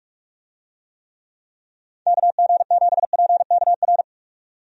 Morse code, heard as a keyed tone, about 46 wpm, 700 Hz. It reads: KG7PKR
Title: USA Call Signs - 6 Characters - 45wpm